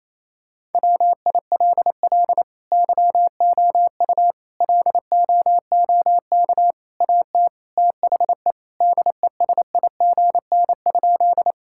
Morse code, heard as a keyed tone, about 28 wpm, 700 Hz. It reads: WILL YOU LOOK AT THE DESIGN?
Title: Sentences from Top 1000 Words - Review Part 5 - 28wpm